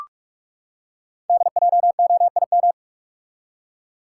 Morse code, heard as a keyed tone, about 44 wpm, 700 Hz. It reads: DJOIM
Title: Code Groups - 5 Letters - 45wpm